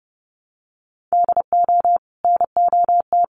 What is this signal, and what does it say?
Morse code, heard as a keyed tone, about 30 wpm, 700 Hz. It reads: DO NOT
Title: Sets of 5 Words 30wpm